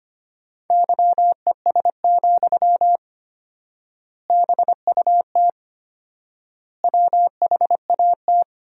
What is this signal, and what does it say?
Morse code, heard as a keyed tone, about 25 wpm, 700 Hz. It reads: YES, BUT WHAT
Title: 4x Word-Spacing - Sentences from Top 500 Words - 25wpm